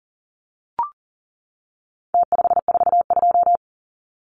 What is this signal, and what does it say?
Morse code, heard as a keyed tone, about 40 wpm, 700 Hz. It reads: T542
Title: Numbers - Contest Style - 4-Digit 40wpm